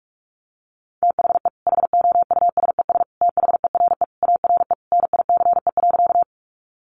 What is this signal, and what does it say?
Morse code, heard as a keyed tone, about 44 wpm, 700 Hz. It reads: THE HOUSES THERE ARE NICE.
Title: Easy Phrases 45wpm